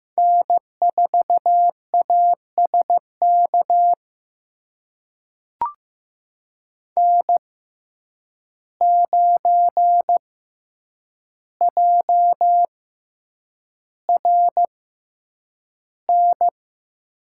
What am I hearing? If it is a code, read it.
Morse code, heard as a keyed tone, about 15 wpm, spaced as for 8 wpm, 700 Hz. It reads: N4ASK N9JRN
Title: USA Call Signs - 5 Characters - 6x Character Spacing - 15wpm